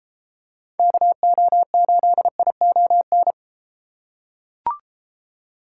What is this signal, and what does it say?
Morse code, heard as a keyed tone, about 32 wpm, 700 Hz. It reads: KO8ION